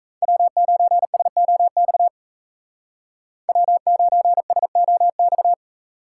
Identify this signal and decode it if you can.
Morse code, heard as a keyed tone, about 33 wpm, 700 Hz. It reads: W9SOX W9SOX
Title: Speed Racing - USA 1x3 Call Signs 28wpm